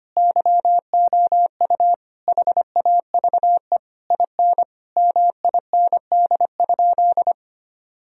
Morse code, heard as a keyed tone, about 25 wpm, 700 Hz. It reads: YOU HAVE IN MIND?
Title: Sentences from Top 1000 Words - Review Part 3 - 25wpm